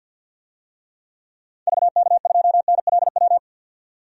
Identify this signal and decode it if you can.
Morse code, heard as a keyed tone, about 47 wpm, 700 Hz. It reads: VK2NLW
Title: DXCC-100 Call Signs - 6 Characters - 50wpm